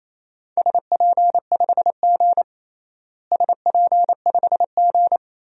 Morse code, heard as a keyed tone, about 28 wpm, 700 Hz. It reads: SP5G SP5G
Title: DXCC-100 Call Signs - 4 Characters - Repeated 3x - 28wpm